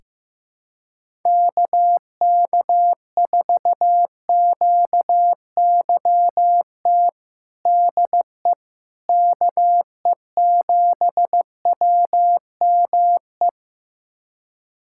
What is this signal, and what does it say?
Morse code, heard as a keyed tone, about 15 wpm, 700 Hz. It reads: KK4QYT DE KE7WME
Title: USA Callsign DE Callsign - 6 Characters - 15wpm